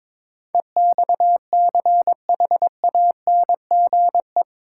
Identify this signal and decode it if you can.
Morse code, heard as a keyed tone, about 22 wpm, 700 Hz. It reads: EXCHANGE
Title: Sentences from Top 1000 Words - Review Part 6 - New Word Drill - 22wpm